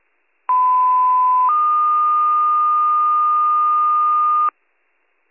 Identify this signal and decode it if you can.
Two-tone page, heard as a one-second tone, then a three-second tone, about 1010 then 1220 Hz, with noise. It is heard from 0.5 to 4.5 s.